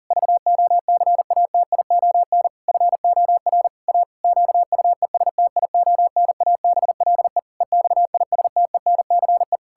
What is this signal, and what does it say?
Morse code, heard as a keyed tone, about 40 wpm, 700 Hz. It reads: VOCATION FOR A QUESTIONABLE EXISTENCE
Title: Pangram Challenge v2 - 40wpm